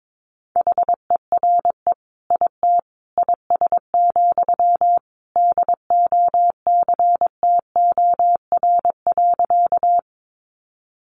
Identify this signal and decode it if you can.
Morse code, heard as a keyed tone, about 22 wpm, 700 Hz. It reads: HERE IT IS, DOCTOR.
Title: Sentences from Top 1500 Words 22wpm